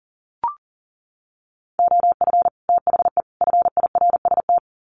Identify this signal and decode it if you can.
Morse code, heard as a keyed tone, about 40 wpm, 700 Hz. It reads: OF THE FIRST
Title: Sets of 3 Words 40wpm